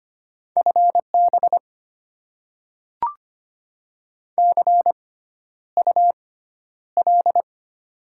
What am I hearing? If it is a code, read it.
Morse code, heard as a keyed tone, about 25 wpm, spaced as for 11 wpm, 700 Hz. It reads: FB CUL
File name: Top 90 QSO Elements - 6x Character Spacing - 25wpm